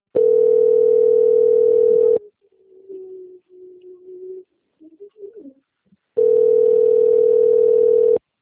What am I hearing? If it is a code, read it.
Ringback tone (North America), 440 + 480 Hz.
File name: out- - - - - .28786